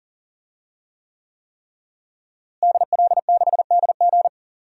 Morse code, heard as a keyed tone, about 40 wpm, 700 Hz. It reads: DL6DG
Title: DXCC-100 Call Signs - 5 Characters - Repeated 3x - 40wpm